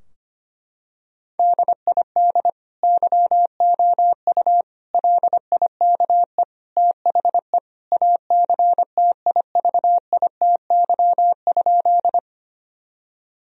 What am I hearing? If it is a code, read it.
Morse code, heard as a keyed tone, about 25 wpm, 700 Hz. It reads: DID YOU LIKE THE ACTIVITY?
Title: Sentences from Top 1000 Words - Review Part 4 - 25wpm